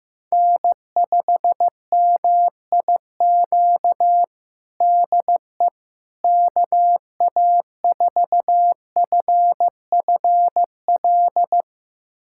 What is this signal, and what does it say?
Morse code, heard as a keyed tone, about 15 wpm, 700 Hz. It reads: N5MIQ DE KA4FFL
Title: USA Callsign DE Callsign 15wpm